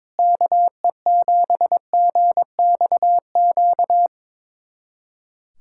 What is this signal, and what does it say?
Morse code, heard as a keyed tone, about 22 wpm, 700 Hz. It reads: KE7GXQ
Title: USA Callsign DE Callsign 22wpm